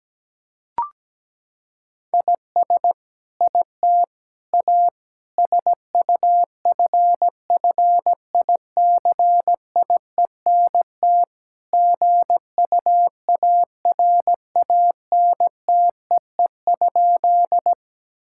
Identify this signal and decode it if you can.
Morse code, heard as a keyed tone, about 17 wpm, 700 Hz. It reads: IS IT A SUFFICIENT GUARANTEE?
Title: Sentences from Top 1600 Words 17wpm